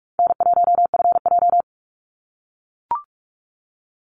Morse code, heard as a keyed tone, about 44 wpm, 700 Hz. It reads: N1FJ
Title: NA SOTA Chaser Callsigns 45wpm